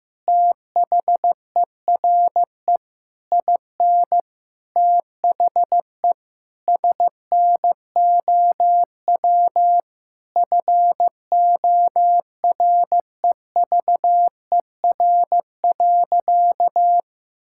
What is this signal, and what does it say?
Morse code, heard as a keyed tone, about 15 wpm, 700 Hz. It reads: THERE IN THE SNOW FOREVER.
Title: Sentences from Top 1600 Words 15wpm